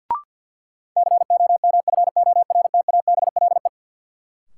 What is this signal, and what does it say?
Morse code, heard as a keyed tone, about 47 wpm, 700 Hz. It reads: COMFORTABLE
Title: Sentences from Top 1000 Words - Review Part 7 - New Word Drill - 50wpm